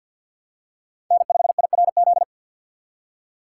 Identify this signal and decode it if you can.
Morse code, heard as a keyed tone, about 51 wpm, 700 Hz. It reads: N5IRZ
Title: USA Call Signs - 3x Character Spacing - 50wpm